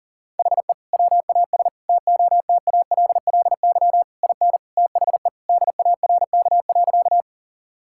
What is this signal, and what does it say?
Morse code, heard as a keyed tone, about 40 wpm, 700 Hz. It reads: HE WAS TOTALLY IN THE DARK.
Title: Sentences from Top 1600 Words 40wpm